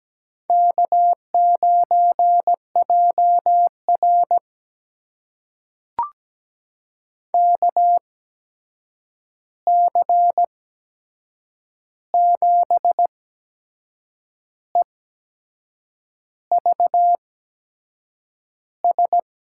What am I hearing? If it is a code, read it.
Morse code, heard as a keyed tone, about 17 wpm, spaced as for 6 wpm, 700 Hz. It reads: K9JR KC7EVS
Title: USA Call Signs - 8x Character Spacing - 17wpm